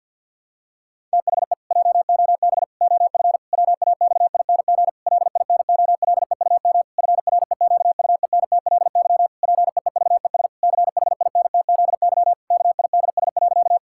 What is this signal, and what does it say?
Morse code, heard as a keyed tone, about 52 wpm, 700 Hz. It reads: THE JOB OF WAXING LINOLEUM FREQUENTLY PEEVES CHINTZY KIDS.
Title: Pangram Challenge v2 - 50wpm